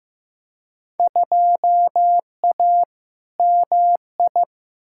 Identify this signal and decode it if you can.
Morse code, heard as a keyed tone, about 15 wpm, 700 Hz. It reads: 2A MI
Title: Field Day Exchanges - 15wpm